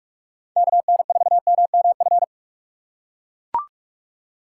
Morse code, heard as a keyed tone, about 43 wpm, 700 Hz. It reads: KN4MMF